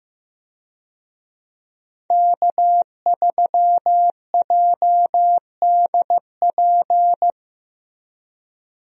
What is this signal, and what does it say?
Morse code, heard as a keyed tone, about 15 wpm, 700 Hz. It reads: K3JDP
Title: USA Call Signs - 5 Characters - Repeated 2x - 15wpm